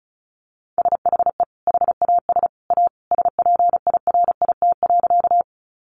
Morse code, heard as a keyed tone, about 34 wpm, 700 Hz. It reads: SHE HAS A SPIRIT.
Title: Sentences from Top 1400 Words 35wpm